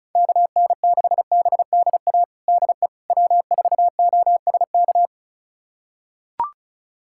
Morse code, heard as a keyed tone, about 36 wpm, 700 Hz. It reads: KN6BDA DE W4OSK
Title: USA Callsign DE Callsign 35wpm